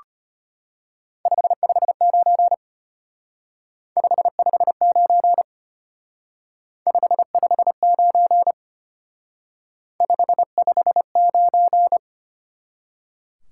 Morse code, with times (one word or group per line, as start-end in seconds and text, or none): 1.3-2.5 559
4.0-5.4 559
6.9-8.5 559
10.0-12.0 559